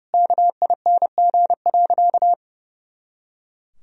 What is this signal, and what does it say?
Morse code, heard as a keyed tone, about 31 wpm, 700 Hz. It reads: KING.
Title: Sentences from Top 400 Words and Abbreviations 30wpm